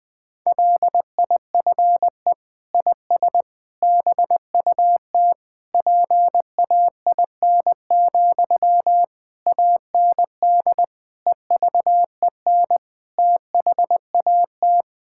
Morse code, heard as a keyed tone, about 21 wpm, 700 Hz. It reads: LIFE IS BUT PAIN, AND EVEN THAT
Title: Sentences from Top 1600 Words 20wpm